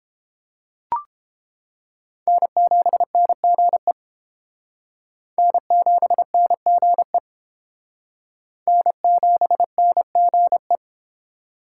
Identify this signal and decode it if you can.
Morse code, heard as a keyed tone, about 29 wpm, 700 Hz. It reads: N7NGE N7NGE N7NGE